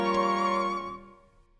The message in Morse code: none